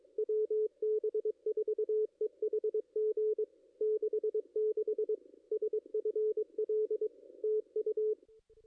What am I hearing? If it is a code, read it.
Morse code, heard as a keyed tone, about 23 wpm, 430 Hz, with radio noise. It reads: WB4EHG 66 SFL TU
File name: contests / ss-cw-k5tr-op-n5rz -2300